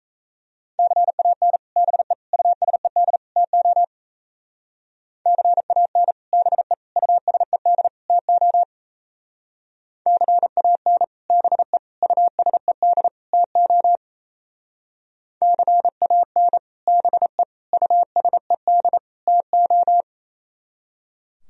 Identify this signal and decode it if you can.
Morse code, heard as a keyed tone, about 35 wpm, 700 Hz. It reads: CAN BE USED TO CAN BE USED TO CAN BE USED TO CAN BE USED TO
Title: Speed-Racing - Sets of 4 Words 28wpm